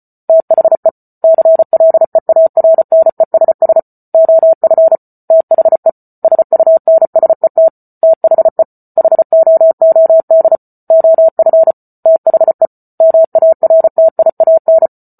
Morse code, heard as a keyed tone, about 34 wpm, 650 Hz. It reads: THE CLEARNESS OF THE SUNSET THE HOOD OF THE MARTIAN